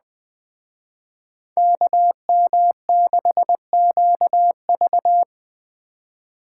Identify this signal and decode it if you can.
Morse code, heard as a keyed tone, about 20 wpm, 700 Hz. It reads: KM6QV
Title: Speed Racing - USA 2x2 Call Signs 17wpm